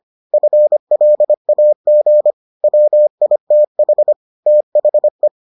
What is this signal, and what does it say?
Morse code, heard as a keyed tone, about 25 wpm, 600 Hz. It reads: FLAG WITH THE